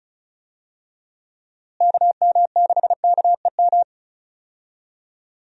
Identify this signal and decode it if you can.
Morse code, heard as a keyed tone, about 34 wpm, 700 Hz. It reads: KM6KEM
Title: USA Call Signs - 6 Characters - Repeated 2x - 35wpm